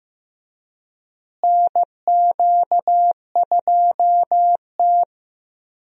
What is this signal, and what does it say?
Morse code, heard as a keyed tone, about 15 wpm, 700 Hz. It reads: NQ2T
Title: USA Call Signs - 4 Characters - Repeated 3x - 15wpm